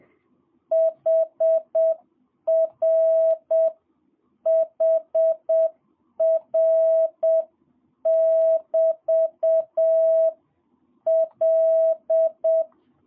Morse code, heard as a keyed tone, about 7 wpm, 660 Hz. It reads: HR HR <BT> L
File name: jh8cbh / 2022cw